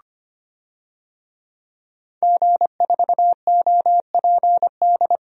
Morse code, heard as a keyed tone, about 25 wpm, 700 Hz. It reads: G4OPD
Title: DXCC-100 Call Signs - 5 Characters - Repeated 3x - 25wpm